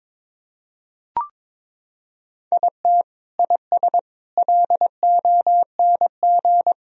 Morse code, heard as a keyed tone, about 22 wpm, 700 Hz. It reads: IT IS LONG
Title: Sentences from Top 1000 Words - Review Part 1 - 22wpm